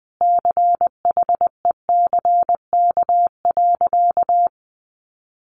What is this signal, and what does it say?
Morse code, heard as a keyed tone, about 19 wpm, 700 Hz. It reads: CHECK.
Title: A Sound Of Thunder 20wpm